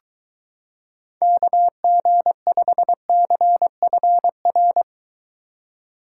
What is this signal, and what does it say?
Morse code, heard as a keyed tone, about 23 wpm, 700 Hz. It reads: KG5CFR